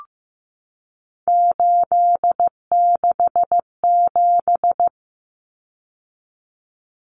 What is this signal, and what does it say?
Morse code, heard as a keyed tone, about 15 wpm, 700 Hz. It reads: 867
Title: Common Cut Numbers - 3-Digit 15wpm